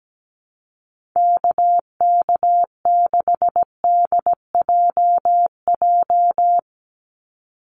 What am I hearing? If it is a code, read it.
Morse code, heard as a keyed tone, about 17 wpm, 700 Hz. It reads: KK6DJJ
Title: USA Call Signs 17wpm